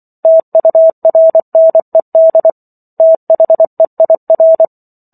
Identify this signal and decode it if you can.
Morse code, heard as a keyed tone, about 24 wpm, 650 Hz. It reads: TURNED THEIR